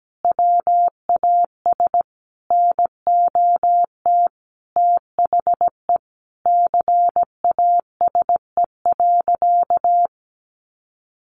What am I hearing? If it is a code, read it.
Morse code, heard as a keyed tone, about 17 wpm, 700 Hz. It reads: WAS NOT THE CASE.
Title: Sentences from Top 1200 Words 17wpm